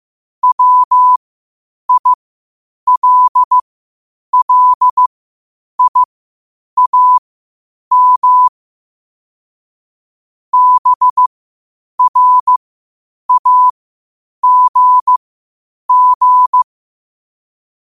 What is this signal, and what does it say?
Morse code, heard as a keyed tone, about 15 wpm, spaced as for 8 wpm, 1000 Hz. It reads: WILLIAM BRAGG
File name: qotdfiles / 8wpm